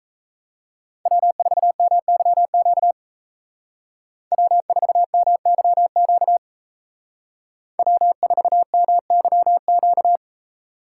Morse code, heard as a keyed tone, about 35 wpm, 700 Hz. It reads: W4MYQ W4MYQ W4MYQ
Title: Speed Racing - USA 1x3 Call Signs 28wpm